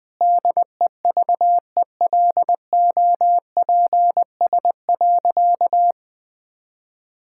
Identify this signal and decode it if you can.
Morse code, heard as a keyed tone, about 20 wpm, 700 Hz. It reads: DEVELOPS.
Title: Sentences with 7-Word Phrases - 20wpm